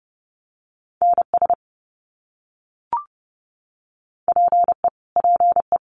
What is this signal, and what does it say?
Morse code, heard as a keyed tone, about 30 wpm, 700 Hz. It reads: NS PE PE
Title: Canadian Provinces and Territories - Repeated 3x - 30wpm